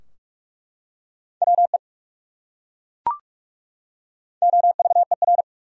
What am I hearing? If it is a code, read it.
Morse code, heard as a keyed tone, about 43 wpm, 700 Hz. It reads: WE OVER
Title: Top 100 Words 45wpm